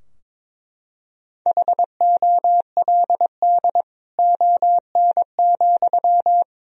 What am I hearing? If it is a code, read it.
Morse code, heard as a keyed tone, about 22 wpm, 700 Hz. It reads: HOLD ON,
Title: Sentences from Top 1000 Words - Review Part 2 - 22wpm